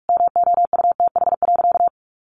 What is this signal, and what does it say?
Morse code, heard as a keyed tone, about 44 wpm, 700 Hz. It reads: MOUTH.
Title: Sentences from Top 1500 Words 45wpm